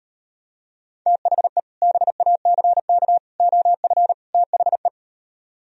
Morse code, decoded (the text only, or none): THE BACK OF THE